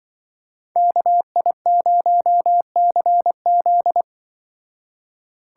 Morse code, heard as a keyed tone, about 24 wpm, 700 Hz. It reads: KI0CZ
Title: Speed Racing - USA 2x2 Call Signs 20wpm